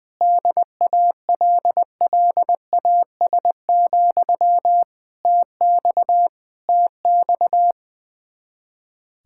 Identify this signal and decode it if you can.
Morse code, heard as a keyed tone, about 20 wpm, 700 Hz. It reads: DALLAS, TX TX
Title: Practice QSO - Exchange 1 - Beginning - 20wpm